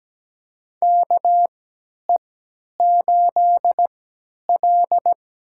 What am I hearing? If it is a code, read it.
Morse code, heard as a keyed tone, about 17 wpm, spaced as for 10 wpm, 700 Hz. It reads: KE8L
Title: USA Call Signs - 3x Character Spacing - 17wpm